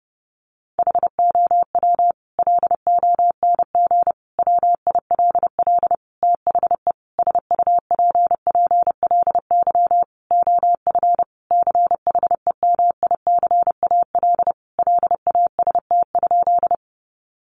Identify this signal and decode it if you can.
Morse code, heard as a keyed tone, about 30 wpm, 700 Hz. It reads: HOW LONG WILL THE SUPPLY OF CHEMICAL LAST?
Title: Sentences from Top 1200 Words 30wpm